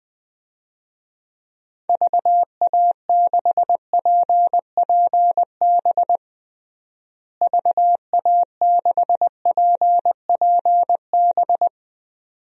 Morse code, heard as a keyed tone, about 21 wpm, 700 Hz. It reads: VA6PPB VA6PPB
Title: DXCC-100 Call Signs - 6 Characters - Repeated 2x - 20wpm